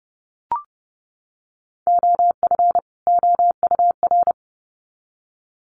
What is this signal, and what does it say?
Morse code, heard as a keyed tone, about 30 wpm, 700 Hz. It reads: OF OUR
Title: Sets of 2 Words 30wpm